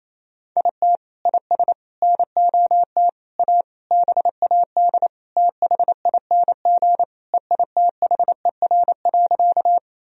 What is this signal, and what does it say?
Morse code, heard as a keyed tone, about 28 wpm, 700 Hz. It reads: IT IS NOT A BAD THING EITHER.
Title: Sentences from Top 1000 Words - Review Part 4 - 28wpm